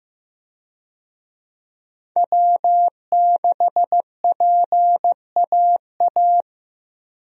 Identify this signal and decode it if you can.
Morse code, heard as a keyed tone, about 15 wpm, 700 Hz. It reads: W6PAA
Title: USA Call Signs - 5 Characters - Repeated 2x - 15wpm